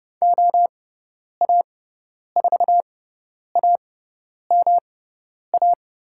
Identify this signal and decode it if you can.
Morse code, heard as a keyed tone, about 30 wpm, spaced as for 10 wpm, 700 Hz. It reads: OA4AMA
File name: Lesson 18g - Letter m - Call Signs 30wpm